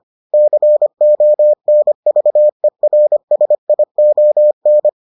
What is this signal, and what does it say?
Morse code, heard as a keyed tone, about 25 wpm, 600 Hz. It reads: CONVERSION